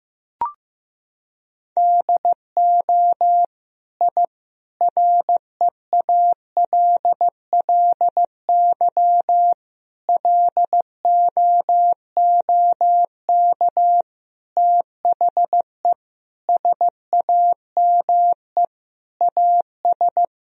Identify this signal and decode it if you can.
Morse code, heard as a keyed tone, about 15 wpm, 700 Hz. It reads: DO I REALLY LOOK THE SAME AS
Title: Sentences from Top 1000 Words - Review Part 2 - 15wpm